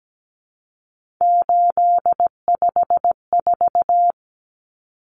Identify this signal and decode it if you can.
Morse code, heard as a keyed tone, about 17 wpm, 700 Hz. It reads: 854
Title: Common Cut Numbers - 3-Digit 17wpm